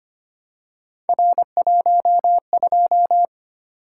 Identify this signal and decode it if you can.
Morse code, heard as a keyed tone, about 25 wpm, 700 Hz. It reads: R12
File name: Three Letter-Number Combinations - 25wpm